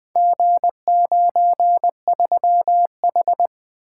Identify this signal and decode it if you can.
Morse code, heard as a keyed tone, about 20 wpm, 700 Hz. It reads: G93H
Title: Four Letter-Number Combinations - 20wpm